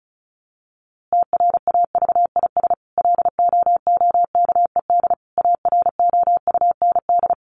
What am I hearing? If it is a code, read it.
Morse code, heard as a keyed tone, about 34 wpm, 700 Hz. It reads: TRAVIS LOOKED AROUND
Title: A Sound Of Thunder 35wpm